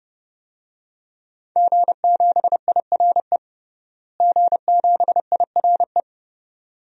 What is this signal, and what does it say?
Morse code, heard as a keyed tone, about 32 wpm, 700 Hz. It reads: G7IRE G7IRE
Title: DXCC-100 Call Signs - 5 Characters - Repeated 2x - 30wpm